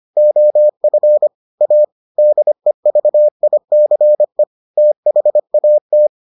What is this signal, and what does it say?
Morse code, heard as a keyed tone, about 25 wpm, 600 Hz. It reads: OF A DEVICE THAT